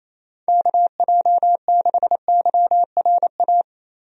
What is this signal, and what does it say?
Morse code, heard as a keyed tone, about 28 wpm, 700 Hz. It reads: KJ6YRA
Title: USA Call Signs - 6 Characters - 28wpm